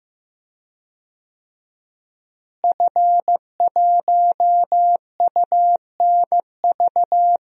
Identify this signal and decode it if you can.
Morse code, heard as a keyed tone, about 15 wpm, 700 Hz. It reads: F1UNV
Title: DXCC-100 Call Signs - 5 Characters - Repeated 3x - 15wpm